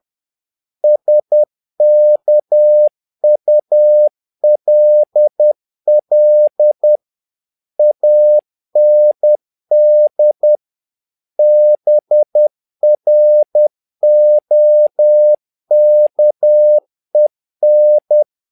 Morse code, heard as a keyed tone, about 10 wpm, 600 Hz. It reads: SKULL AND BROKEN
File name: learning center / tower trouble / tower trouble 10wpm